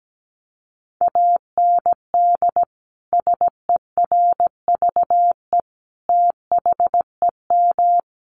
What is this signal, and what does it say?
Morse code, heard as a keyed tone, about 17 wpm, 700 Hz. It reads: AND SERVE THEM
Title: Sentences from Top 400 Words 17wpm